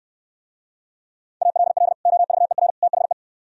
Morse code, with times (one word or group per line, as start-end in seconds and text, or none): 1.4-1.9 ALL
2.1-2.7 OVER
2.8-3.1 THE